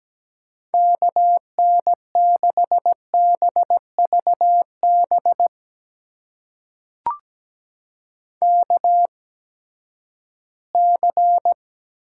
Morse code, heard as a keyed tone, about 17 wpm, spaced as for 12 wpm, 700 Hz. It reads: KN6BVB KC
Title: USA Call Signs - 6 Characters - 8x Character Spacing - 17wpm